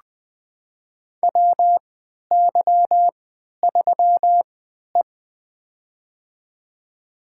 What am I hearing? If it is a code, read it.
Morse code, heard as a keyed tone, about 20 wpm, spaced as for 11 wpm, 700 Hz. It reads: WY3E
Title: USA Call Signs - 4 Characters - 3x Character Spacing - 20wpm